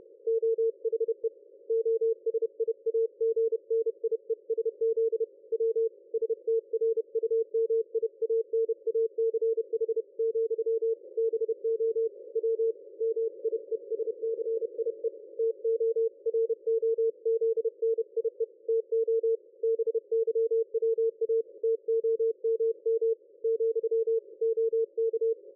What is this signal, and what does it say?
Morse code, heard as a keyed tone, about 30 wpm, 450 Hz. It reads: OHE OSIAGNIESZ W STRUMIANACH, BO W MIESCIE TOROZNIE TO BYWA TOMM , OK
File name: dlazaawansowanych / nagraniazpasma / ExpertSDR2 line out 23.01.2020 14.22.53